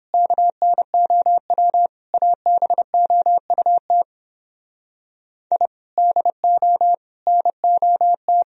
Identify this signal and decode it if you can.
Morse code, heard as a keyed tone, about 31 wpm, 700 Hz. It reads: KNOW ABOUT I DO NOT